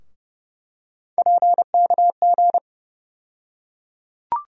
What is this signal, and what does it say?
Morse code, heard as a keyed tone, about 30 wpm, 700 Hz. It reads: PKG
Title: USA Call Sign Postfixes - 3 Letters 30wpm